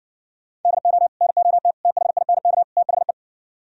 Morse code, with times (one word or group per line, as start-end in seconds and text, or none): 0.6-1.1 DO
1.2-1.7 NOT
1.9-2.6 THINK
2.8-3.1 THE